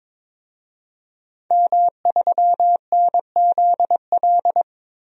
Morse code, heard as a keyed tone, about 22 wpm, 700 Hz. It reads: M3NZL
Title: DXCC-100 Call Signs - 5 Characters - 22wpm